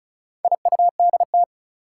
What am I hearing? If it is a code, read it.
Morse code, heard as a keyed tone, about 36 wpm, 700 Hz. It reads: IUDT
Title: Four Challenging Letter-Number Combinations 35wpm